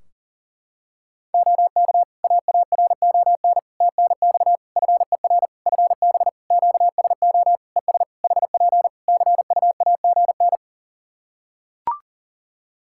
Morse code, heard as a keyed tone, about 40 wpm, 700 Hz. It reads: OK AARON TNX FER FB QSO ES HP CUAGN
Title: K1USN SST QSO Practice 40wpm